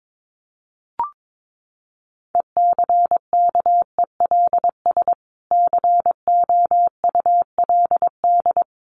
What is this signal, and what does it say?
Morse code, heard as a keyed tone, about 22 wpm, 700 Hz. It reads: ECKELS COULD
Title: A Sound Of Thunder 22wpm